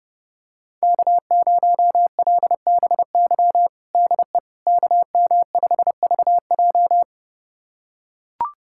Morse code, heard as a keyed tone, about 30 wpm, 700 Hz. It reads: K0LBY DE KM5VJ
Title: USA Callsign DE Callsign - 5 Characters - 30wpm